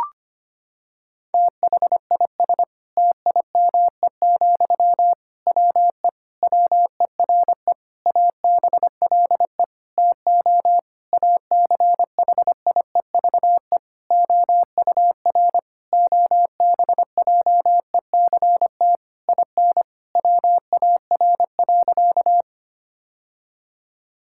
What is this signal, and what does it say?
Morse code, heard as a keyed tone, about 25 wpm, 700 Hz. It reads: THIS TIME, WE WERE ABLE TO ACHIEVE OUR OBJECT IN WAR.
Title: Sentences from Top 1000 Words - Review Part 6 - 25wpm